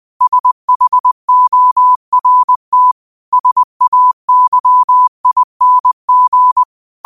Morse code, heard as a keyed tone, about 20 wpm, 1000 Hz. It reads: SHORT SAYING